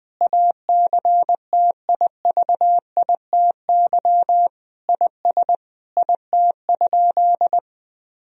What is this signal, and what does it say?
Morse code, heard as a keyed tone, about 20 wpm, 700 Hz. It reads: ACTIVITY IS IT?
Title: Sentences from Top 1000 Words - Review Part 4 - 20wpm